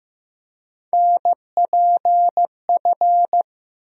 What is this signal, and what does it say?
Morse code, heard as a keyed tone, about 15 wpm, 700 Hz. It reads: NPF
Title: USA Call Sign Postfixes - 3 Letters 15wpm